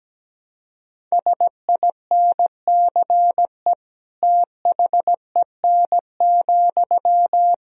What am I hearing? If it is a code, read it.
Morse code, heard as a keyed tone, about 17 wpm, 700 Hz. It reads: SINCE THEN,
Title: Sentences from Top 1000 Words - Review Part 1 - 17wpm